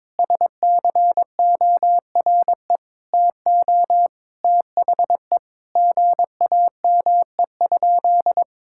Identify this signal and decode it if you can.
Morse code, heard as a keyed tone, about 22 wpm, 700 Hz. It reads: SCORE TO THE GAME?
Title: Sentences from Top 1600 Words 22wpm